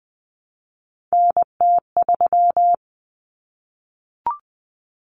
Morse code, heard as a keyed tone, about 20 wpm, 700 Hz. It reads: NT3
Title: Common Cut Numbers - 3-Digit 20wpm